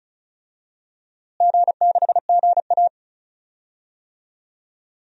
Morse code, heard as a keyed tone, about 35 wpm, 700 Hz. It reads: G6GA
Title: DXCC-100 Call Signs - 4 Characters - Repeated 3x - 35wpm